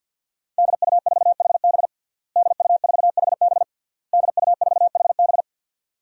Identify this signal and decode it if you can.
Morse code, heard as a keyed tone, about 51 wpm, 700 Hz. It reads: DU4HB DU4HB DU4HB
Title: DXCC-100 Call Signs - 5 Characters - Repeated 3x - 50wpm